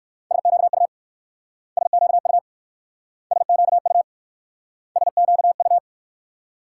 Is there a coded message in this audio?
Yes: Morse code from 0.3 to 5.8 s, heard as a keyed tone, about 58 wpm, 690 Hz. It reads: SQU SQU SQU SQU